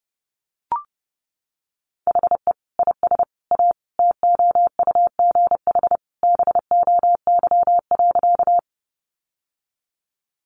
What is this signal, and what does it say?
Morse code, heard as a keyed tone, about 30 wpm, 700 Hz. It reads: HE IS A TOUGH BOY.
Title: Sentences from Top 1200 Words 30wpm